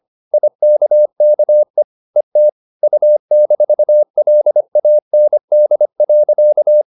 Morse code, heard as a keyed tone, about 25 wpm, 600 Hz. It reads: IKKE ET U-LAND.
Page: Ferie 25 wpm